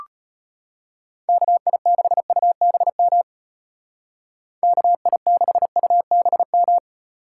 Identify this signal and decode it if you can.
Morse code, heard as a keyed tone, about 32 wpm, 700 Hz. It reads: KI6UBM KI6UBM